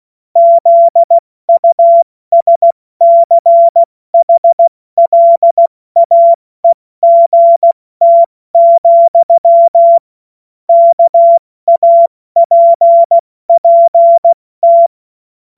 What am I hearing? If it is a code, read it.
Morse code, heard as a keyed tone, about 16 wpm, 680 Hz. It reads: ZUSCHLAEGT, KAPPT
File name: Morsekurs / Witze / Witz 19 Dezember 2023 70 F80